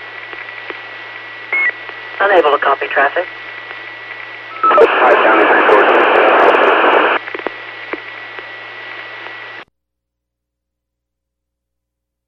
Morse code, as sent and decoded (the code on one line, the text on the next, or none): none